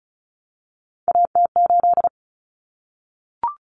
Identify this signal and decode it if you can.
Morse code, heard as a keyed tone, about 34 wpm, 700 Hz. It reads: AT8
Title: Common Cut Numbers - 3-Digit 35wpm